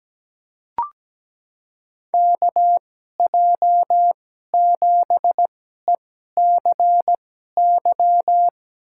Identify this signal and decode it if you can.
Morse code, heard as a keyed tone, about 17 wpm, spaced as for 12 wpm, 700 Hz. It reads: KJ7ECY
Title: USA Call Signs - 6 Characters - 2x Character Spacing - 17wpm